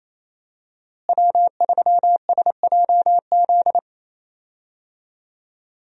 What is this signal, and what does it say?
Morse code, heard as a keyed tone, about 28 wpm, 700 Hz. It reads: W3SJZ
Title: USA Call Signs - 5 Characters - Repeated 2x - 28wpm